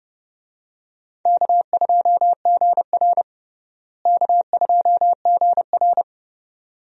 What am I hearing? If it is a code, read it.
Morse code, heard as a keyed tone, about 32 wpm, 700 Hz. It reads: K2GR K2GR
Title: USA Call Signs - 4 Characters - Repeated 2x - 30wpm